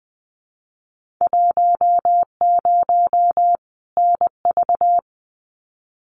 Morse code, heard as a keyed tone, about 20 wpm, 700 Hz. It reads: 10 NV
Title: Winter Field Day Exchanges 20wpm